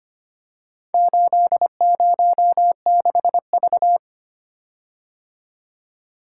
Morse code, heard as a keyed tone, about 25 wpm, 690 Hz. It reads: 806V
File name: practice / 25wpm / Four Letter-Number Combinations - 25wpm